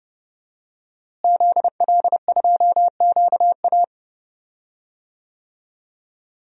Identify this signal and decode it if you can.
Morse code, heard as a keyed tone, about 32 wpm, 700 Hz. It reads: ZL2QA
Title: DXCC-100 Call Signs - 5 Characters - Repeated 2x - 30wpm